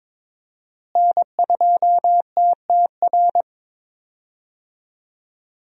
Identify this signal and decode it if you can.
Morse code, heard as a keyed tone, about 23 wpm, 700 Hz. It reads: N2TTR
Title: USA Call Signs - 5 Characters - Repeated 2x - 22wpm